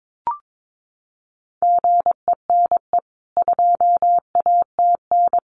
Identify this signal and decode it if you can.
Morse code, heard as a keyed tone, about 22 wpm, 700 Hz. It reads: GENE 2ATN
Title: CWT Exchange 22wpm